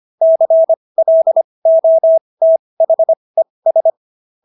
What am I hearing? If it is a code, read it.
Morse code, heard as a keyed tone, about 25 wpm, spaced as for 20 wpm, 650 Hz. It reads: CLOTHES